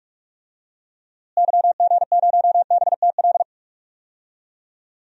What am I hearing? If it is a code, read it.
Morse code, heard as a keyed tone, about 42 wpm, 700 Hz. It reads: YG0BTL
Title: DXCC-100 Call Signs - 6 Characters - Repeated 2x - 45wpm